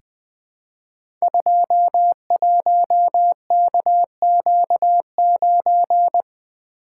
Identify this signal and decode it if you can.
Morse code, heard as a keyed tone, about 20 wpm, 700 Hz. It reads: 21KQ9
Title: Code Groups - 5 Letter:Numbers - 20wpm